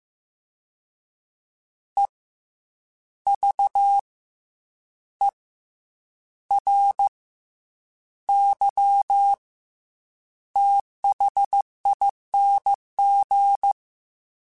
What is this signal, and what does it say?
Morse code, heard as a keyed tone, about 15 wpm, 800 Hz. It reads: E V E R Y THING
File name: Special INGs / Special INGs-mp3s / SlowToFaster ing @13